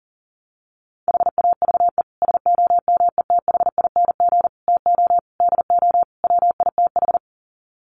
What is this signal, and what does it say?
Morse code, heard as a keyed tone, about 40 wpm, 700 Hz. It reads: HAVE SOMETHING TO DO WITH
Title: Sets of 5 Words Encore 40wpm